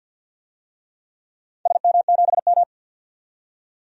Morse code, heard as a keyed tone, about 53 wpm, 700 Hz. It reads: SM7M
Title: DXCC-100 Call Signs - 4 Characters - 50wpm